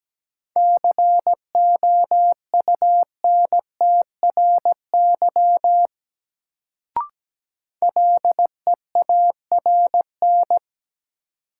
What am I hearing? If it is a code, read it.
Morse code, heard as a keyed tone, about 17 wpm, 700 Hz. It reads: COUNTRY LEARN
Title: Sentences from Top 300 Words - New Word Drill - 17wpm